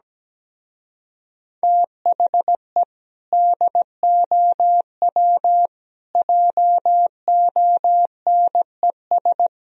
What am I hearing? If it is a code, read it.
Morse code, heard as a keyed tone, about 17 wpm, 700 Hz. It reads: THE DOW JONES
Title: Speed-Racing - Sets of 5 Words 17wpm